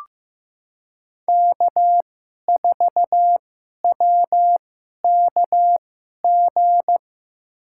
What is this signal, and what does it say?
Morse code, heard as a keyed tone, about 15 wpm, spaced as for 11 wpm, 700 Hz. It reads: K4WKG
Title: USA Call Signs - 5 Characters - 2x Character Spacing - 15wpm